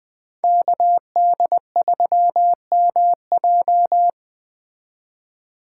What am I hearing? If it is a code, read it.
Morse code, heard as a keyed tone, about 20 wpm, 700 Hz. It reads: KD3MJ
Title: Speed Racing - USA 2x2 Call Signs 17wpm